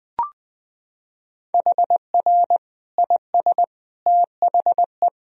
Morse code, heard as a keyed tone, about 20 wpm, 700 Hz. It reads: HR IS THE
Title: Sentences from Top 300 Words and Abbreviations 20wpm